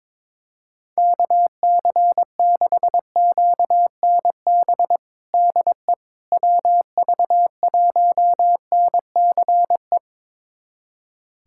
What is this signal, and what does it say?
Morse code, heard as a keyed tone, about 21 wpm, 700 Hz. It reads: KC6QNB DE WV1NCE
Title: USA Callsign DE Callsign - 6 Characters - 22wpm